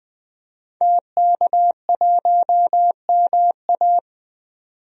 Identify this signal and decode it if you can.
Morse code, heard as a keyed tone, about 20 wpm, 700 Hz. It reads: TK1MA
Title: DXCC-100 Call Signs - 5 Characters - 20wpm